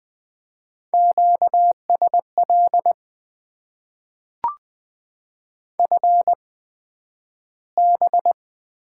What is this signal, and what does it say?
Morse code, heard as a keyed tone, about 20 wpm, spaced as for 11 wpm, 700 Hz. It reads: QSL FB
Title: Top 90 QSO Elements - 8x Character Spacing - 20wpm